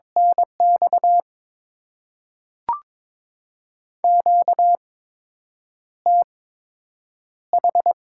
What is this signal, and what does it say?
Morse code, heard as a keyed tone, about 22 wpm, spaced as for 8 wpm, 700 Hz. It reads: NX QTH
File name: Top 20 QSO Elements - 8x Character Spacing - 22wpm